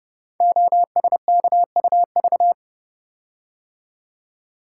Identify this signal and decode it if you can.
Morse code, heard as a keyed tone, about 30 wpm, 700 Hz. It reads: OSKUV
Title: Code Groups - 5 Letters - 30wpm